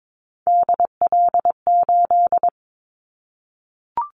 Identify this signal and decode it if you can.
Morse code, heard as a keyed tone, about 22 wpm, 700 Hz. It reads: DL8
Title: Callsign Prefixes for Top 25 DXCC - 22wpm